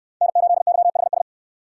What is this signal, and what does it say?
Morse code, heard as a keyed tone, about 68 wpm, 700 Hz. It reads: N8YHS